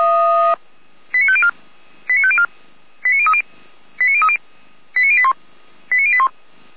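Five-tone selective call (ZVEI), heard as quick runs of five tones, tones of about 70 ms. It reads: 89504 89504 89030 89030 89081 89081
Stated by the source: ZVEI five-tone selective call 89050 received on an SDR.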